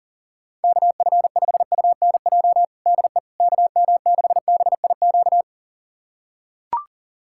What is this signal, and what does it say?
Morse code, heard as a keyed tone, about 40 wpm, 700 Hz. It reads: KF5UNJ DE KM6BIQ
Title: USA Callsign DE Callsign 40wpm